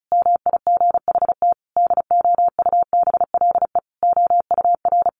Morse code, heard as a keyed tone, about 34 wpm, 700 Hz. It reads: MIGHT DOUBLE OUR
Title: Sentences from Top 1400 Words 35wpm